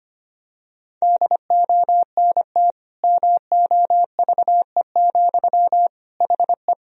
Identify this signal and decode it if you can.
Morse code, heard as a keyed tone, about 25 wpm, 700 Hz. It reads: DONT MOVE, HE
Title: Sentences from Top 1000 Words - Review Part 1 - 25wpm